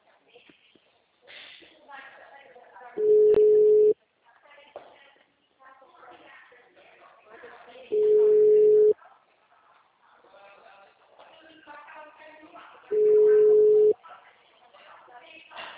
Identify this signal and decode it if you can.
Ringback tone (Europe, 425 Hz), with speech.